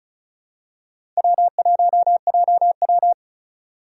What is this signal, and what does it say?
Morse code, heard as a keyed tone, about 36 wpm, 700 Hz. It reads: W1JW
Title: USA Call Signs - 2x Character Spacing - 35wpm